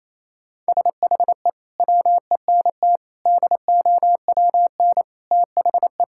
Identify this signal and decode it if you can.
Morse code, heard as a keyed tone, about 28 wpm, 700 Hz. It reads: SHE WENT DOWN THE
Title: Sentences from Top 1000 Words - Review Part 5 - 28wpm